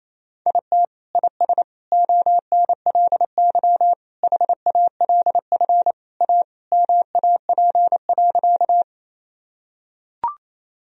Morse code, heard as a keyed tone, about 28 wpm, 700 Hz. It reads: IT IS ONLY HALF A MAP.
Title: Sentences from Top 1000 Words - Review Part 2 - 28wpm